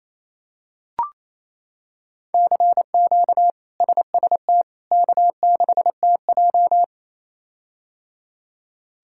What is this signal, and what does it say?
Morse code, heard as a keyed tone, about 27 wpm, 700 Hz. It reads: CQ SST K6TJ
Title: K1USN SST QSO Practice - 28wpm